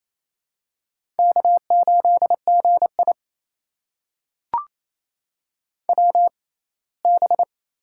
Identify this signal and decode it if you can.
Morse code, heard as a keyed tone, about 27 wpm, spaced as for 19 wpm, 700 Hz. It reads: K8GI WB
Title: USA Call Signs - 6x Character Spacing - 28wpm